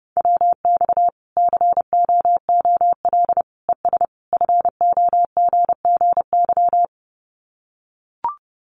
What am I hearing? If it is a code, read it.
Morse code, heard as a keyed tone, about 30 wpm, 700 Hz. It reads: WX COOL ES FOGGY
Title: Common QSO Phrases 30wpm